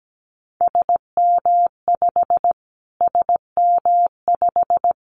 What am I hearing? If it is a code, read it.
Morse code, heard as a keyed tone, about 17 wpm, 700 Hz. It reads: SM5 SM5
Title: Callsign Prefixes for Top 25 DXCC - Repeated 3x - 17wpm